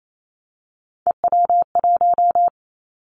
Morse code, heard as a keyed tone, about 28 wpm, 700 Hz. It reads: EW1
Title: Callsign Prefixes for Top 100 DXCC - 28wpm